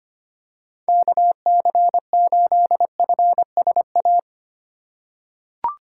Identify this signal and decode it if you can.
Morse code, heard as a keyed tone, about 25 wpm, 700 Hz. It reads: KC8FSA
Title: USA Call Signs - 6 Characters - 2x Character Spacing - 25wpm